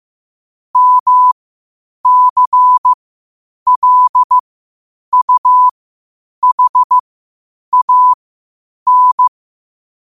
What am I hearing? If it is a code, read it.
Morse code, heard as a keyed tone, about 15 wpm, spaced as for 8 wpm, 1000 Hz. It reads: MCLUHAN
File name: qotdfiles / 8wpm